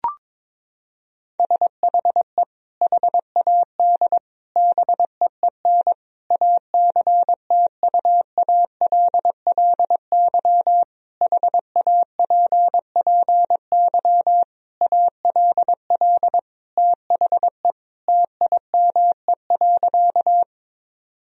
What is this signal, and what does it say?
Morse code, heard as a keyed tone, about 22 wpm, 700 Hz. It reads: SHE HAD BEEN ACTUALLY HAPPY ALL THE TIME.
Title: Sentences from Top 1000 Words - Review Part 4 - 22wpm